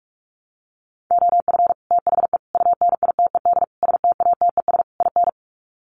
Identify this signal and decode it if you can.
Morse code, heard as a keyed tone, about 44 wpm, 700 Hz. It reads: OF THE UNITED STATES IN
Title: Sets of 5 Words Encore 45wpm